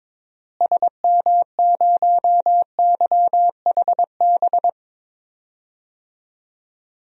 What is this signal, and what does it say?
Morse code, heard as a keyed tone, about 22 wpm, 700 Hz. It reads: SM0YHB
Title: DXCC-100 Call Signs - 6 Characters - Repeated 2x - 22wpm